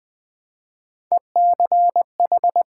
Morse code, heard as a keyed tone, about 20 wpm, 700 Hz. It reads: ECH
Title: Build A Word - 1 Letter At A Time - 4-Letter Words - 20wpm